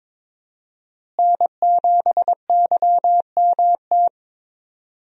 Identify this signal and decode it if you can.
Morse code, heard as a keyed tone, about 22 wpm, 700 Hz. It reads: N7YMT
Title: USA Call Signs - 5 Characters - Repeated 2x - 22wpm